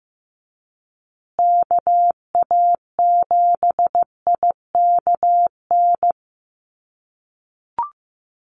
Morse code, heard as a keyed tone, about 15 wpm, 700 Hz. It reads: KA7IKN
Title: USA Call Signs Repeated 2x 15wpm